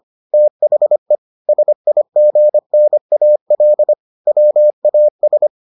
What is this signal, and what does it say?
Morse code, heard as a keyed tone, about 25 wpm, 600 Hz. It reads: THE SIGNAL WAS